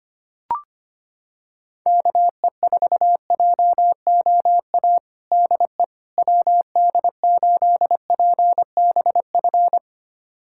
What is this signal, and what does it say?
Morse code, heard as a keyed tone, about 25 wpm, 700 Hz. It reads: KE4JOA DE WD8PBF
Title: USA Callsign DE Callsign - 6 Characters - 25wpm